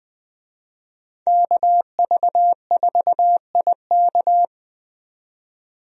Morse code, heard as a keyed tone, about 20 wpm, 700 Hz. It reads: KV4IK
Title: USA Call Signs - 5 Characters - 20wpm